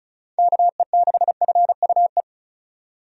Morse code, heard as a keyed tone, about 36 wpm, 700 Hz. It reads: KE6FUE
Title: USA Call Signs - 6 Characters - 35wpm